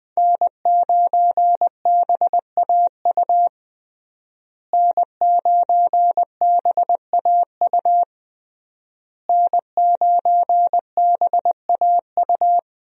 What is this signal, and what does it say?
Morse code, heard as a keyed tone, about 20 wpm, 700 Hz. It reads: N9BAU N9BAU N9BAU
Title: USA Call Signs - 5 Characters - Repeated 3x - 20wpm